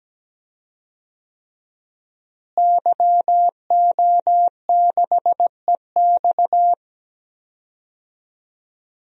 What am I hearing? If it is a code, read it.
Morse code, heard as a keyed tone, about 17 wpm, 700 Hz. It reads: YO6EX
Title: DXCC-100 Call Signs - 5 Characters - Repeated 3x - 17wpm